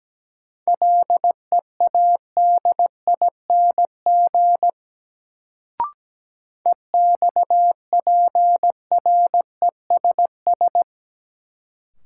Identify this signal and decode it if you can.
Morse code, heard as a keyed tone, about 17 wpm, 700 Hz. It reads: LEADING EXPRESS
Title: Sentences from Top 1100 Words - New Word Drill - 17wpm